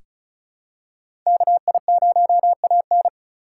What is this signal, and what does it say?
Morse code, heard as a keyed tone, about 36 wpm, 700 Hz. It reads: KI0AN
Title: USA Call Signs - 5 Characters - 4x Character Spacing - 35wpm